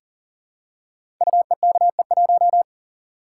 Morse code, heard as a keyed tone, about 38 wpm, 690 Hz. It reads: UEKE1
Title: Code Groups - 5 Letter:Numbers - 40wpm